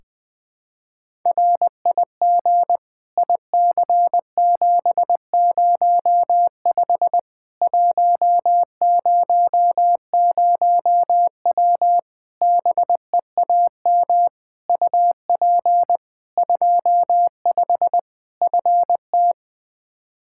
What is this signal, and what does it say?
Morse code, heard as a keyed tone, about 21 wpm, 700 Hz. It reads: RIG IC705 100W BEAM UP 25 FT
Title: K1USN SST QSO Practice 20wpm